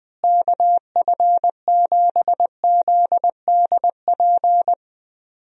Morse code, heard as a keyed tone, about 21 wpm, 700 Hz. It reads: KF7ZDP
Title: USA Call Signs - 6 Characters - Repeated 3x - 20wpm